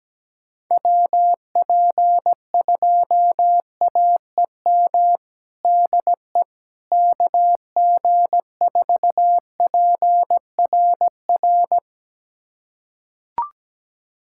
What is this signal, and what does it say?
Morse code, heard as a keyed tone, about 17 wpm, 700 Hz. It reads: WP2AEM DE KG4PRR
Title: USA Callsign DE Callsign - 6 Characters - 17wpm